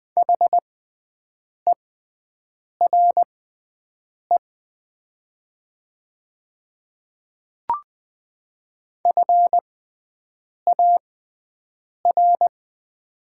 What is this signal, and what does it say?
Morse code, heard as a keyed tone, about 20 wpm, spaced as for 7 wpm, 700 Hz. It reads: HERE FAR
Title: 6x Character Spacing - Top 300 Words - 20wpm